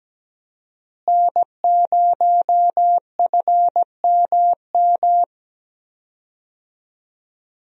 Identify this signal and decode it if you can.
Morse code, heard as a keyed tone, about 17 wpm, 700 Hz. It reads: N0FMM
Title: USA Call Signs - 5 Characters - Repeated 3x - 17wpm